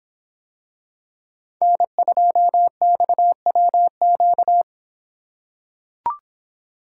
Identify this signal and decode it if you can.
Morse code, heard as a keyed tone, about 26 wpm, 700 Hz. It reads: N2XWQ